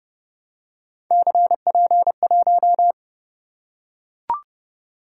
Morse code, heard as a keyed tone, about 29 wpm, 700 Hz. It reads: CP1